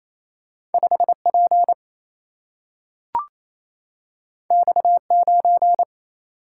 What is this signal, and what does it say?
Morse code, heard as a keyed tone, about 28 wpm, 700 Hz. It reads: 5 P X 9
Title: Single-Letter Single-Number 28wpm